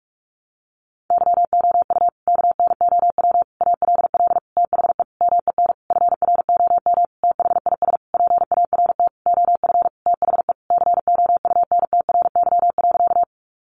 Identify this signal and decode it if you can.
Morse code, heard as a keyed tone, about 44 wpm, 700 Hz. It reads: YOU KNOW ALL THE MEN FROM THIS PART OF THE COUNTRY.
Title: Sentences from Top 300 Words 45wpm